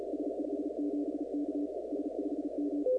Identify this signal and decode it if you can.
Morse code, heard as a keyed tone, about 37 wpm, 290 Hz. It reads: S57K S5M
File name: contests / 2019-cw-wpx-kt5j-k5pi- -2330 0-16